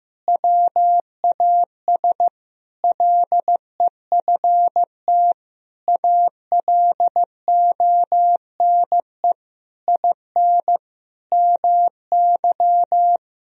Morse code, heard as a keyed tone, about 16 wpm, 700 Hz. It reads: WAS LEFT ALONE IN MY
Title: Sentences from Top 1000 Words - Review Part 1 - 15wpm